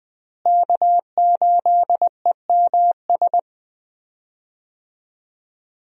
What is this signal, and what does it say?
Morse code, heard as a keyed tone, about 21 wpm, 700 Hz. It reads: K8EMS
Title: USA Call Signs - 5 Characters - Repeated 3x - 20wpm